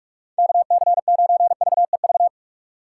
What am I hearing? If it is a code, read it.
Morse code, heard as a keyed tone, about 45 wpm, 700 Hz. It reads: KC9VEV
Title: USA Call Signs - 6 Characters - 3x Character Spacing - 45wpm